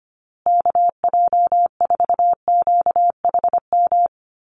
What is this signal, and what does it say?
Morse code, heard as a keyed tone, about 25 wpm, 700 Hz. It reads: KJ4QHM
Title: USA Call Signs Repeated 2x 25wpm